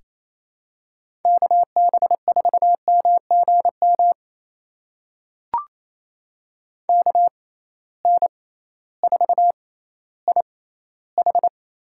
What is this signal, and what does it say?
Morse code, heard as a keyed tone, about 28 wpm, spaced as for 15 wpm, 700 Hz. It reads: KB4MGM KN4IH
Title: USA Call Signs - 6 Characters - 6x Character Spacing - 28wpm